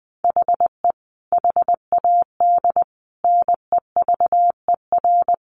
Morse code, heard as a keyed tone, about 20 wpm, 700 Hz. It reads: HE HAD NEVER
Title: Sentences from Top 1300 Words 20wpm